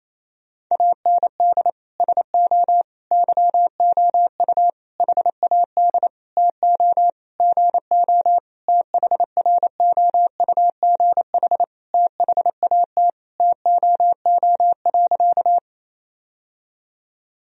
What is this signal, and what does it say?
Morse code, heard as a keyed tone, about 28 wpm, 700 Hz. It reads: AND SO YOU HAD TO GO THROUGH THAT TOO.
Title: Sentences from Top 1000 Words - Review Part 1 - 28wpm